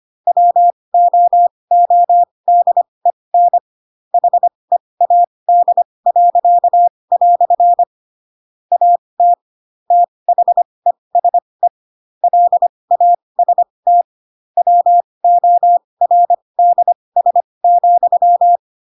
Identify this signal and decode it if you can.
Morse code, heard as a keyed tone, about 25 wpm, spaced as for 20 wpm, 700 Hz. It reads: WOODEN HEAD." AT THESE LAST WORDS,
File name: Pinocchio / Pinocchio ch4 15 25 ik1hge